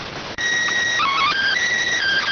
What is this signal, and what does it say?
SSTV (slow-scan television), PD 290.